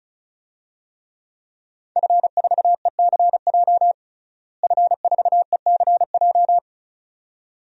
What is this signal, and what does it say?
Morse code, heard as a keyed tone, about 34 wpm, 700 Hz. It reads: F4ECJ F4ECJ
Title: DXCC-100 Call Signs - 5 Characters - Repeated 2x - 35wpm